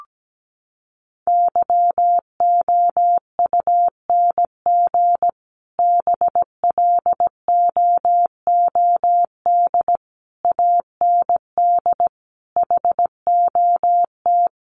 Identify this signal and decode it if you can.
Morse code, heard as a keyed tone, about 17 wpm, 700 Hz. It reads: YOUNG BLOOD AND HOT
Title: Sentences from Top 1100 Words 17wpm